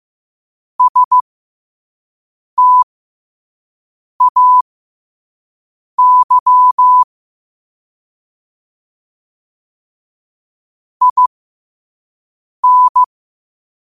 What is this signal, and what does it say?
Morse code, heard as a keyed tone, about 15 wpm, spaced as for 5 wpm, 1000 Hz. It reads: STAY IN
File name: qotdfiles / 5wpm